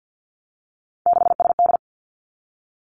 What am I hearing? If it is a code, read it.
Morse code, heard as a keyed tone, about 50 wpm, 700 Hz. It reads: 6SD
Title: Three Challenging Letter-Number Combinations 50wpm